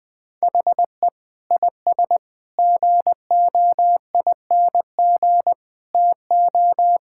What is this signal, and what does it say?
Morse code, heard as a keyed tone, about 20 wpm, 700 Hz. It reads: HE IS GOING TO
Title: Sentences with 5-Word Phrases - 20wpm